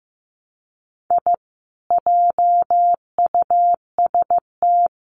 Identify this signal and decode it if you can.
Morse code, heard as a keyed tone, about 15 wpm, 700 Hz. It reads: I JUST
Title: Sets of 2 Words 15wpm